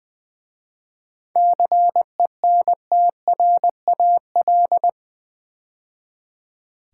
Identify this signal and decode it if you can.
Morse code, heard as a keyed tone, about 20 wpm, 700 Hz. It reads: CENTRAL
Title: 7 Letter Words - First 3 Letters Given - 20wpm